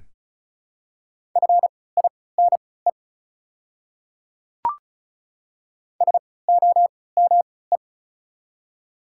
Morse code, heard as a keyed tone, about 34 wpm, spaced as for 20 wpm, 700 Hz. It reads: FINE SOME
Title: 3x Character Spacing - Top 500 Words - 35wpm